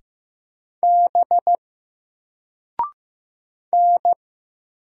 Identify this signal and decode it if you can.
Morse code, heard as a keyed tone, about 15 wpm, 700 Hz. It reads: B N
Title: Single Letter-Number Alt 15wpm